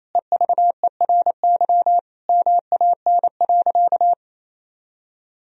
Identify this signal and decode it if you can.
Morse code, heard as a keyed tone, about 28 wpm, 700 Hz. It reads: EVERY MAN.
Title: Sentences from Top 1000 Words - Review Part 1 - 28wpm